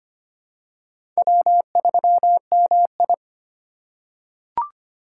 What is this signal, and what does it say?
Morse code, heard as a keyed tone, about 25 wpm, 700 Hz. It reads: W3MI
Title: USA Call Signs - 2x Character Spacing - 25wpm